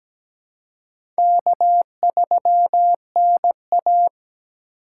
Morse code, heard as a keyed tone, about 17 wpm, 700 Hz. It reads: K3NA
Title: USA Call Signs - 4 Characters - 3x Character Spacing - 17wpm